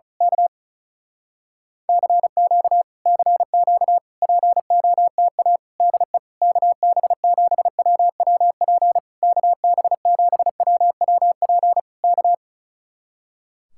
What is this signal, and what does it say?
Morse code, heard as a keyed tone, about 33 wpm, 700 Hz. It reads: K CQ CQ POTA DE KB7WWP KB7WWP K
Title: Speed-Racing Common QSO Phrases Encore - 35wpm